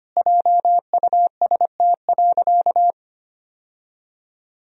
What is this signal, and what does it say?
Morse code, heard as a keyed tone, about 25 wpm, 700 Hz. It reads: JUST.
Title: Sentences from Top 1000 Words - Review Part 4 - 25wpm